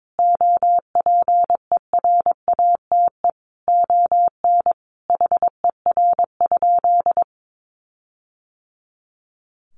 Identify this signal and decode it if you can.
Morse code, heard as a keyed tone, about 22 wpm, 700 Hz. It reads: OPERATE ON HER?
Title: Sentences from Top 1500 Words 22wpm